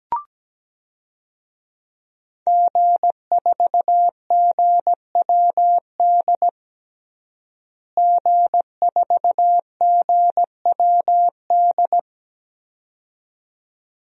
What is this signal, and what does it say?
Morse code, heard as a keyed tone, about 17 wpm, 700 Hz. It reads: G4GWD G4GWD
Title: DXCC-100 Call Signs - 5 Characters - Repeated 2x - 17wpm